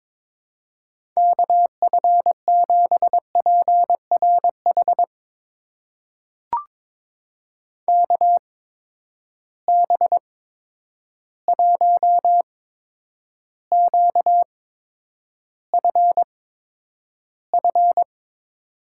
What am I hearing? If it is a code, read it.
Morse code, heard as a keyed tone, about 22 wpm, spaced as for 9 wpm, 700 Hz. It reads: KF7PRH KB1QFF
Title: USA Call Signs - 6 Characters - 8x Character Spacing - 22wpm